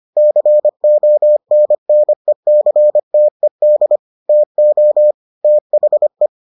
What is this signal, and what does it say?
Morse code, heard as a keyed tone, about 25 wpm, 600 Hz. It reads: CONNECTED TO THE